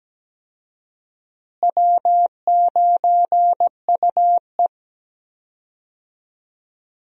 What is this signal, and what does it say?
Morse code, heard as a keyed tone, about 17 wpm, 700 Hz. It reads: W9UE
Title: USA Call Signs - 4 Characters - Repeated 3x - 17wpm